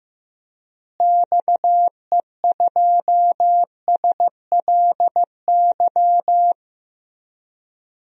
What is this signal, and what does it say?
Morse code, heard as a keyed tone, about 15 wpm, 700 Hz. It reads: XE2SLY
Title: DXCC-100 Call Signs - 6 Characters - Repeated 3x - 15wpm